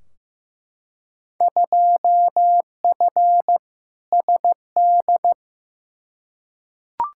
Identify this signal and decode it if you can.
Morse code, heard as a keyed tone, about 15 wpm, 700 Hz. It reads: 2F SD
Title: Field Day Exchanges Encore - 15wpm